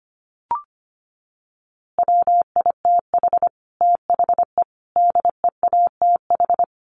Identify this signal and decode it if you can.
Morse code, heard as a keyed tone, about 25 wpm, 700 Hz. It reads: WITH THE DEATH
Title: A Sound Of Thunder 25wpm